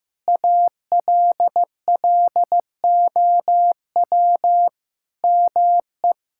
Morse code, heard as a keyed tone, about 15 wpm, 700 Hz. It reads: ALLOW ME
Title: Sentences from Top 1000 Words - Review Part 5 - 15wpm